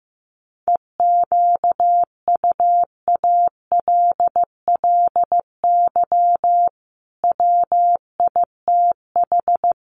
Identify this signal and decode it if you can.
Morse code, heard as a keyed tone, about 15 wpm, 700 Hz. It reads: EQUALLY WITH
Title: Sentences from Top 1400 Words 15wpm